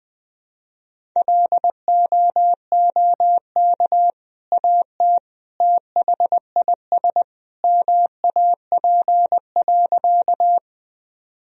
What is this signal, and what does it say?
Morse code, heard as a keyed tone, about 20 wpm, 700 Hz. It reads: LOOK AT THIS MAP.
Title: Sentences from Top 1000 Words - Review Part 2 - 20wpm